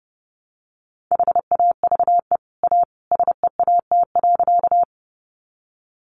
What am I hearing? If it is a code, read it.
Morse code, heard as a keyed tone, about 30 wpm, 700 Hz. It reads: HAVE A SEAT.
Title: Sentences from Top 1500 Words 30wpm